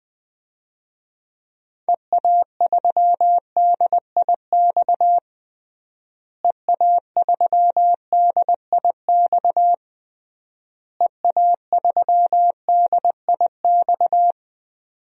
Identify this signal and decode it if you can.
Morse code, heard as a keyed tone, about 20 wpm, 700 Hz. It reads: EA3DIX EA3DIX EA3DIX
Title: DXCC-100 Call Signs - 6 Characters - Repeated 3x - 20wpm